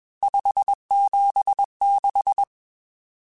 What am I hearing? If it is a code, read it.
Morse code, heard as a keyed tone, about 21 wpm, 800 Hz. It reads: 576
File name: RSTs / RST-mp3s / Various RST @20